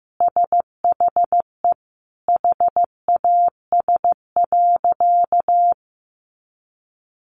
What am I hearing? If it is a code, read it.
Morse code, heard as a keyed tone, about 14 wpm, 700 Hz. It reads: SHE HAS.
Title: Sentences from Top 500 Words 15wpm